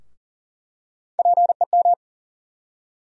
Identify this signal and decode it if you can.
Morse code, heard as a keyed tone, about 40 wpm, 700 Hz. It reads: PEM
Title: USA Call Sign Postfixes - 3 Letters 40wpm